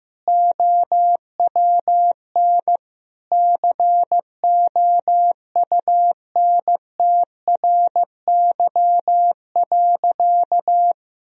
Morse code, heard as a keyed tone, about 15 wpm, 700 Hz. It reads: OWN COUNTRY.
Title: Sentences from Top 1000 Words - Review Part 1 - 15wpm